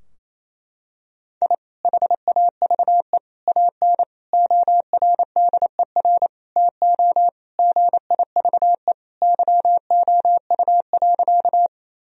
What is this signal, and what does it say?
Morse code, heard as a keyed tone, about 28 wpm, 700 Hz. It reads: I HAVE AN ORDER TO GIVE YOU.
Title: Sentences from Top 1000 Words - Review Part 2 - 28wpm